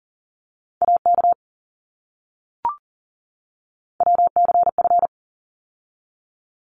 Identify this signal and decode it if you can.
Morse code, heard as a keyed tone, about 40 wpm, 700 Hz. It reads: AK WCF
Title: ARRL Sections 40wpm